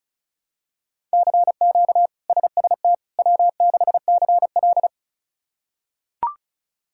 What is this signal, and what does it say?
Morse code, heard as a keyed tone, about 36 wpm, 700 Hz. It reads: CQ SST W6CL
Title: K1USN SST QSO Practice - 35wpm